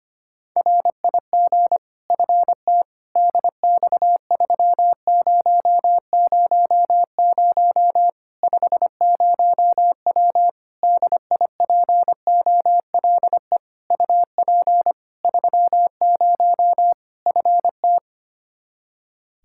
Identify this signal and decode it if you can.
Morse code, heard as a keyed tone, about 25 wpm, 700 Hz. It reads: RIG FT DX3000 50W DIPOLE UP 30 FT
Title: Practice QSO - Exchange 2 - Info - 25wpm